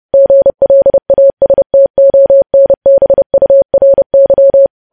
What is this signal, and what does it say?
Morse code, heard as a keyed tone, about 30 wpm, 550 Hz. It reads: GLASTONBURY
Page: news headlines in morse code at 30 wpm